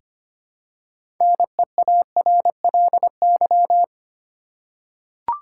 Morse code, heard as a keyed tone, about 25 wpm, 700 Hz. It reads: NEARLY
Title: Six Letter Words - 25wpm